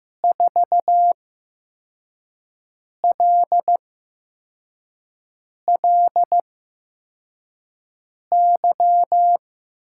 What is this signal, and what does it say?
Morse code, heard as a keyed tone, about 15 wpm, spaced as for 4 wpm, 700 Hz. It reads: 4LLY
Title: USA Call Signs - 5 Characters - 8x Character Spacing - 15wpm